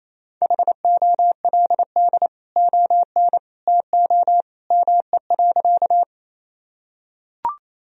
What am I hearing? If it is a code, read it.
Morse code, heard as a keyed tone, about 28 wpm, 700 Hz. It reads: HOLD ON TO ME.
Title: Sentences from Top 1000 Words - Review Part 2 - 28wpm